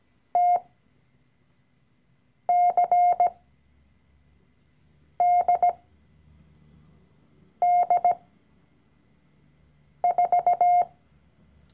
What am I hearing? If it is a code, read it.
Morse code, heard as a keyed tone, about 17 wpm, spaced as for 4 wpm, 710 Hz. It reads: TCDD4
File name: CW-Online-Kurs / Anfänger / Lektion 012